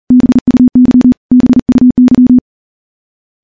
Morse code, heard as a keyed tone, about 40 wpm, 250 Hz. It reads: BUY BUY
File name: morse-training-40WPM